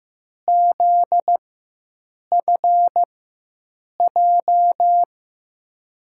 Morse code, heard as a keyed tone, about 15 wpm, spaced as for 7 wpm, 700 Hz. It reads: ZFJ
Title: USA Call Signs - 6 Characters - 4x Character Spacing - 15wpm